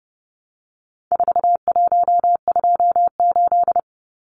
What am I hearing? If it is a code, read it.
Morse code, heard as a keyed tone, about 30 wpm, 700 Hz. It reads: 4128
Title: Numbers - 4-Digit 30wpm